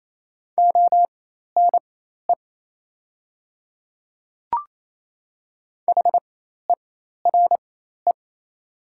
Morse code, heard as a keyed tone, about 27 wpm, spaced as for 13 wpm, 700 Hz. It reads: ONE HERE
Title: 4x Character Spacing - Top 200 Words - 28wpm